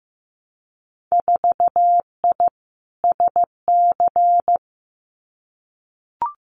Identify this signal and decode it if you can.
Morse code, heard as a keyed tone, about 15 wpm, 700 Hz. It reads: 4I SC
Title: Winter Field Day Exchanges 15wpm